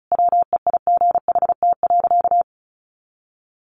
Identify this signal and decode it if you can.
Morse code, heard as a keyed tone, about 34 wpm, 700 Hz. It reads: WEIGHT.
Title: A Sound Of Thunder 35wpm